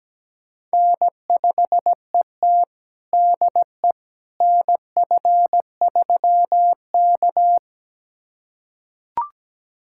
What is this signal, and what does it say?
Morse code, heard as a keyed tone, about 17 wpm, 700 Hz. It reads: N5ET DE NF3K
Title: USA Callsign DE Callsign - 4 Characters - 17wpm